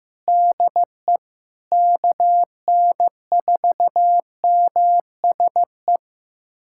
Morse code, heard as a keyed tone, about 15 wpm, 700 Hz. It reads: DE KN4MSE
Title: USA Callsign DE Callsign 15wpm